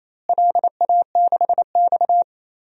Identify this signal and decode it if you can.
Morse code, heard as a keyed tone, about 27 wpm, 700 Hz. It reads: LA6X
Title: DXCC-100 Call Signs - 4 Characters - Repeated 3x - 28wpm